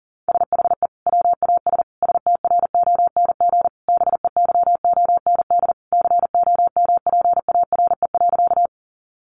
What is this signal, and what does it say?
Morse code, heard as a keyed tone, about 40 wpm, 700 Hz. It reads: SHE WAS STRONG BEYOND COMPARE.
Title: Sentences from Top 1300 Words 40wpm